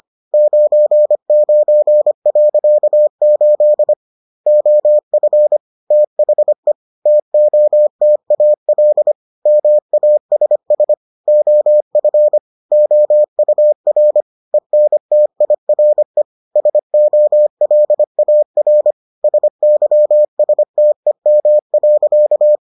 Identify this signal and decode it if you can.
Morse code, heard as a keyed tone, about 25 wpm, 600 Hz. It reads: 99.8 OF THE TOTAL MASS OF OUR ENTIRE SOLAR SYSTEM.